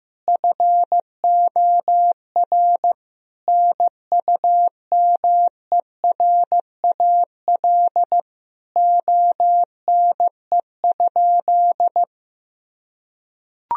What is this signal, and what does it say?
Morse code, heard as a keyed tone, about 16 wpm, 700 Hz. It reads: FOR NUMERAL ONE?
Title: Sentences from Top 1000 Words - Review Part 2 - 15wpm